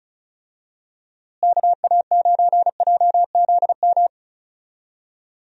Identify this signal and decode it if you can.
Morse code, heard as a keyed tone, about 36 wpm, 700 Hz. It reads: KA9JZM
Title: USA Call Signs - 6 Characters - Repeated 2x - 35wpm